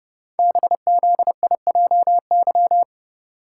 Morse code, heard as a keyed tone, about 30 wpm, 690 Hz. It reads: BZIJY
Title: Code Groups - 5 Letters - 30wpm